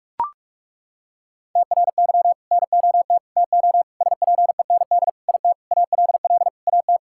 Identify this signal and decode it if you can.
Morse code, heard as a keyed tone, about 41 wpm, 700 Hz. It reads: TRY NOT TO SPEND IT ALL AT
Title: Sentences from Top 1000 Words - Review Part 5 - 45wpm